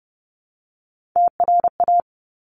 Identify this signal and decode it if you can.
Morse code, heard as a keyed tone, about 30 wpm, 700 Hz. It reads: TRA
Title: Top Three Letter Combinations in Words 30wpm